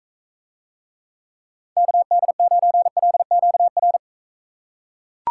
Morse code, heard as a keyed tone, about 40 wpm, 700 Hz. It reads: KD9LQR